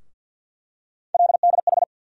Morse code, heard as a keyed tone, about 52 wpm, 700 Hz. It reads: LDH
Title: USA Call Sign Postfixes - 3 Letters 50wpm